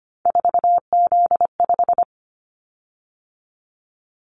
Morse code, heard as a keyed tone, about 25 wpm, 700 Hz. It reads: 4Z5
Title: Callsign Prefixes for Top 50 DXCC - 25wpm